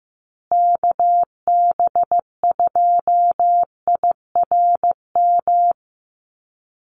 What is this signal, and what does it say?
Morse code, heard as a keyed tone, about 15 wpm, 700 Hz. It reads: KB2IRM
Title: USA Call Signs Repeated 2x 15wpm